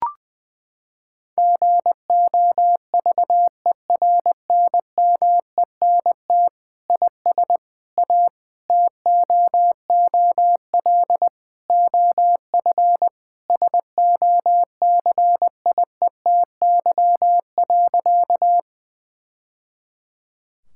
Morse code, heard as a keyed tone, about 21 wpm, 700 Hz. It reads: GOVERNMENT IS A TOOL OF SOCIETY.
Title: Sentences from Top 1000 Words - Review Part 6 - 20wpm